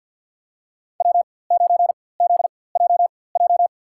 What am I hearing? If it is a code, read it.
Morse code, heard as a keyed tone, about 47 wpm, spaced as for 23 wpm, 700 Hz. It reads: W9ZJJ
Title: USA Call Signs - 5 Characters - 4x Character Spacing - 50wpm